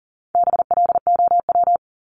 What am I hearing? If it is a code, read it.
Morse code, heard as a keyed tone, about 40 wpm, 700 Hz. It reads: BLOW
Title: Top 1751-2000 Words 40wpm